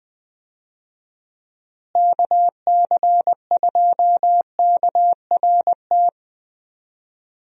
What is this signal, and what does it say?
Morse code, heard as a keyed tone, about 20 wpm, 700 Hz. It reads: KC2KRT
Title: USA Call Signs - 6 Characters - Repeated 2x - 20wpm